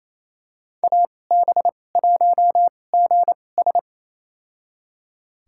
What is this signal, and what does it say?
Morse code, heard as a keyed tone, about 28 wpm, spaced as for 20 wpm, 700 Hz. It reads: AB1GS
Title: USA Call Signs - 5 Characters - 2x Character Spacing - 28wpm